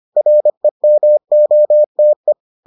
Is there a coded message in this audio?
Yes: Morse code from 0.2 to 2.3 s, heard as a keyed tone, about 25 wpm, 600 Hz. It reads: REMOTE